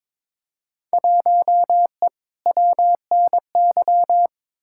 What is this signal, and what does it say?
Morse code, heard as a keyed tone, about 22 wpm, 700 Hz. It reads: 1E WNY
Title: Field Day Exchanges - 22wpm